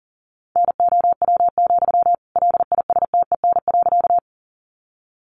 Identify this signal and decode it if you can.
Morse code, heard as a keyed tone, about 40 wpm, 700 Hz. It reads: NOW, LISTEN.
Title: Sentences from Top 400 Words 40wpm